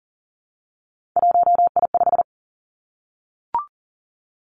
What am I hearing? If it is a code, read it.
Morse code, heard as a keyed tone, about 40 wpm, 700 Hz. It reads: 1I5
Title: Three Challenging Letter-Number Combinations 40wpm